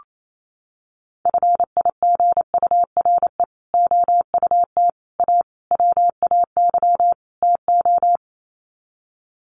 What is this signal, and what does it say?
Morse code, heard as a keyed tone, about 28 wpm, 700 Hz. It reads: FIGURE OUT A WAY TO
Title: Sets of 5 Words Encore2 28wpm